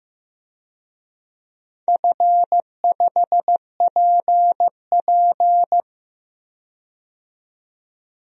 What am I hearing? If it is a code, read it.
Morse code, heard as a keyed tone, about 15 wpm, 700 Hz. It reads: F5PP
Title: DXCC-100 Call Signs - 4 Characters - 15wpm